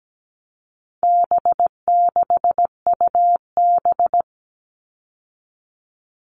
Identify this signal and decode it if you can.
Morse code, heard as a keyed tone, about 17 wpm, 700 Hz. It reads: B6UB
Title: All Cut Numbers - 4-Digit 17wpm